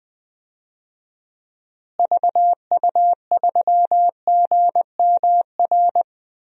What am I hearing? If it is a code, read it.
Morse code, heard as a keyed tone, about 20 wpm, 700 Hz. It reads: VU3GMR
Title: DXCC-100 Call Signs - 6 Characters - Repeated 2x - 20wpm